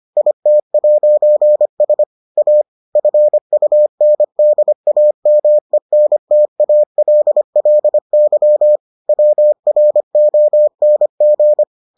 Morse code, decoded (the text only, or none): IT'S A FUNDAMENTALLY WRONG